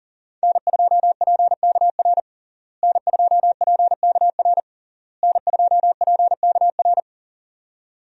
Morse code, heard as a keyed tone, about 40 wpm, 700 Hz. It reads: N2PKR N2PKR N2PKR
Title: USA Call Signs - 5 Characters - Repeated 3x - 40wpm